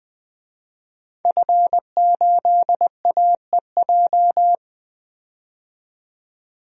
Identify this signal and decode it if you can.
Morse code, heard as a keyed tone, about 20 wpm, 700 Hz. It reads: F8AEJ
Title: DXCC-100 Call Signs - 5 Characters - 20wpm